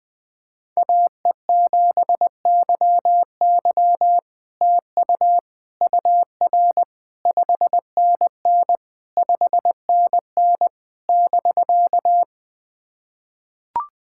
Morse code, heard as a keyed tone, about 20 wpm, 700 Hz. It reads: AE7YY TU UR 5NN 5NN <BK>
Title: SOTA QSO Practice 20wpm